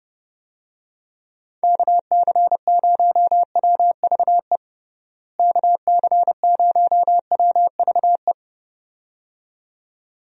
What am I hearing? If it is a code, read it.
Morse code, heard as a keyed tone, about 30 wpm, 700 Hz. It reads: KC0WVE KC0WVE
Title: USA Call Signs - 6 Characters - Repeated 2x - 30wpm